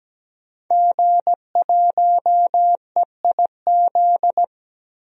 Morse code, heard as a keyed tone, about 17 wpm, 700 Hz. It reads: G1EIZ
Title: DXCC-100 Call Signs - 5 Characters - 17wpm